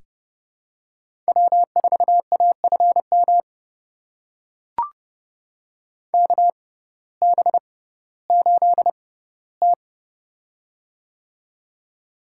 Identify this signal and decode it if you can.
Morse code, heard as a keyed tone, about 30 wpm, spaced as for 16 wpm, 700 Hz. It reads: W4AFM KB8T
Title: USA Call Signs - 6x Character Spacing - 30wpm